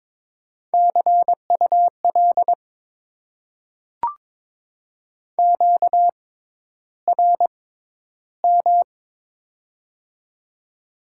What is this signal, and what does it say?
Morse code, heard as a keyed tone, about 22 wpm, spaced as for 11 wpm, 700 Hz. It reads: CUL QRM
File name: Top 90 QSO Elements - 6x Character Spacing - 22wpm